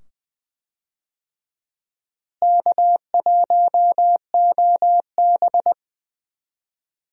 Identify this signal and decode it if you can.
Morse code, heard as a keyed tone, about 20 wpm, 700 Hz. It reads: K1OB
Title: USA Call Signs - 4 Characters - Repeated 2x - 20wpm